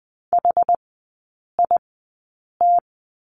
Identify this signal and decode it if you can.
Morse code, heard as a keyed tone, about 20 wpm, spaced as for 8 wpm, 700 Hz. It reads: HIT
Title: 172 Lesson 28e - Number 9 - Three Characters 20wpm